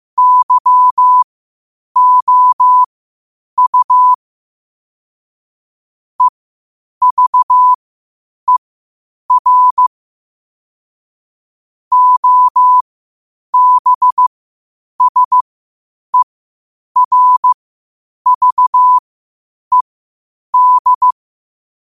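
Morse code, heard as a keyed tone, about 15 wpm, spaced as for 8 wpm, 1000 Hz. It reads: YOU EVER OBSERVED
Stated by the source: Morse code quotes at 8 words per minute from QOTD at 8 WPM.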